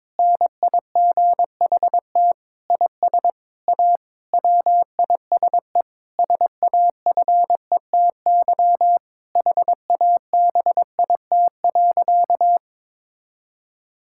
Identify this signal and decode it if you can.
Morse code, heard as a keyed tone, about 22 wpm, 700 Hz. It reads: NIGHT IS A WISE SAFETY HABIT.
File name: Difficult Sentences with Hints - 22wpm